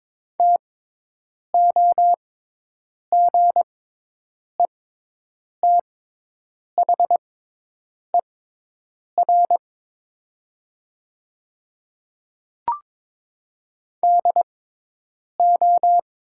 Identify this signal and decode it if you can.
Morse code, heard as a keyed tone, about 22 wpm, spaced as for 8 wpm, 700 Hz. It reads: TOGETHER DO
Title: 6x Character Spacing - Top 300 Words - 22wpm